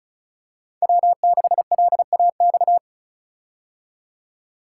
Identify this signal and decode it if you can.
Morse code, heard as a keyed tone, about 36 wpm, 700 Hz. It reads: W6LAX
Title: K1USN SST QSO Practice - 35wpm